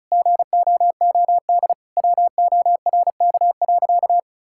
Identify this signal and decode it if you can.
Morse code, heard as a keyed tone, about 34 wpm, 700 Hz. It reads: GOOD WORK.
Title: Sentences from Top 1000 Words - Review Part 5 - 35wpm